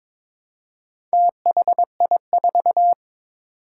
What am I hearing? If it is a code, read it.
Morse code, heard as a keyed tone, about 22 wpm, 700 Hz. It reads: THI4
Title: Four Challenging Letter-Number Combinations 22wpm